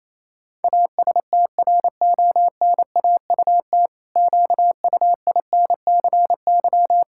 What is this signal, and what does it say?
Morse code, heard as a keyed tone, about 28 wpm, 700 Hz. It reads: ASTRONAUT QUINCY
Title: Pangram Challenge v2 - 28wpm